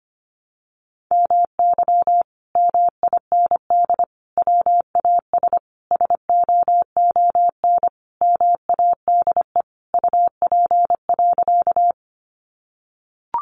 Morse code, heard as a keyed tone, about 25 wpm, 700 Hz. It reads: MY MIND WAS SOON MADE UP.
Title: Sentences from Top 500 Words 25wpm